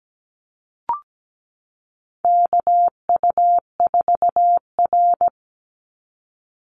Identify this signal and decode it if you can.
Morse code, heard as a keyed tone, about 17 wpm, 700 Hz. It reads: KU4R
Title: NA SOTA Chaser Callsigns 17wpm